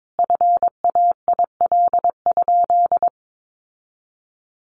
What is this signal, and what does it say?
Morse code, heard as a keyed tone, about 22 wpm, 700 Hz. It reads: FAIL?
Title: Sentences from Top 1100 Words 22wpm